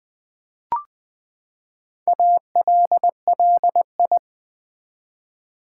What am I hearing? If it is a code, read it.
Morse code, heard as a keyed tone, about 20 wpm, 700 Hz. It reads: ALLI
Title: Build A Word - 4 Letters At A Time - 8-Letter Words - 20wpm